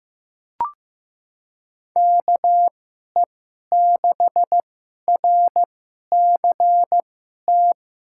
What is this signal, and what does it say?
Morse code, heard as a keyed tone, about 15 wpm, spaced as for 11 wpm, 700 Hz. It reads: KE6RCT
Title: USA Call Signs - 6 Characters - 2x Character Spacing - 15wpm